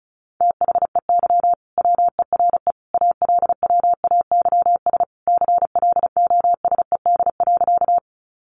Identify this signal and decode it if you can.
Morse code, heard as a keyed tone, about 34 wpm, 700 Hz. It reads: THEY WERE ALWAYS CLOSED.
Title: Sentences from Top 1400 Words 35wpm